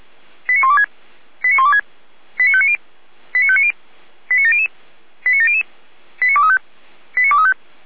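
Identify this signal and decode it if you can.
Five-tone selective call (ZVEI), heard as quick runs of five tones, tones of about 70 ms. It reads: 89126 89126 89590 89590 89700 89700 89235 89235
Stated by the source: ZVEI five-tone selective call received on an SDR.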